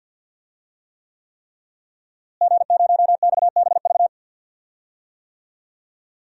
Morse code, heard as a keyed tone, about 49 wpm, 700 Hz. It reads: G0XBV
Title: DXCC-100 Call Signs - 5 Characters - Repeated 3x - 50wpm